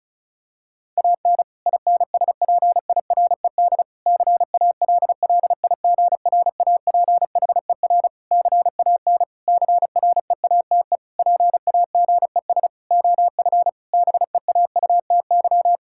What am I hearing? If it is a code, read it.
Morse code, heard as a keyed tone, about 36 wpm, 700 Hz. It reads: AN INSPIRED CALLIGRAPHER CAN CREATE PAGES OF BEAUTY
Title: Pangram Challenge v2 - 35wpm